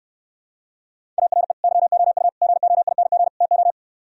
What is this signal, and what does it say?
Morse code, heard as a keyed tone, about 67 wpm, 700 Hz. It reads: ARE YOU GOING TO